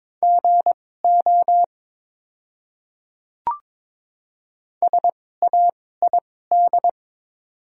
Morse code, heard as a keyed tone, about 22 wpm, spaced as for 16 wpm, 690 Hz. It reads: GO SAID
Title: 2x Character Spacing - Top 500 Words - 22wpm